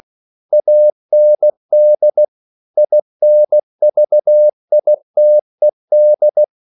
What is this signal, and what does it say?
Morse code, heard as a keyed tone, about 16 wpm, 600 Hz. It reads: AND INVITED